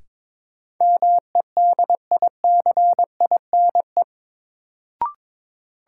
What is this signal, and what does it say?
Morse code, heard as a keyed tone, about 22 wpm, 700 Hz. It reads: MEDICINE
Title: Sentences from Top 900 Words - New Word Drill - 22wpm